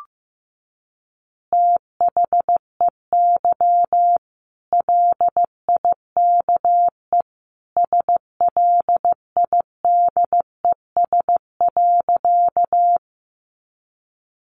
Morse code, heard as a keyed tone, about 15 wpm, 700 Hz. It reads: THEY LIKE SLIDES.
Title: Easy Phrases 15wpm